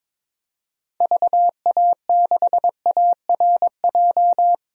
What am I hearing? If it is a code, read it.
Morse code, heard as a keyed tone, about 22 wpm, 700 Hz. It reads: VA6ARJ
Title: DXCC-100 Call Signs - 6 Characters - Repeated 2x - 22wpm